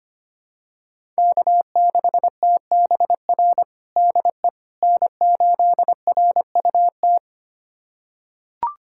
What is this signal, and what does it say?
Morse code, heard as a keyed tone, about 25 wpm, 700 Hz. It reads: K6TBR DE N8RUT
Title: USA Callsign DE Callsign - 5 Characters - 25wpm